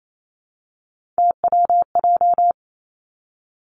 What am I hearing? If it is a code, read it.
Morse code, heard as a keyed tone, about 28 wpm, 690 Hz. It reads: TWJ
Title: Three Challenging Letter-Number Combinations 28wpm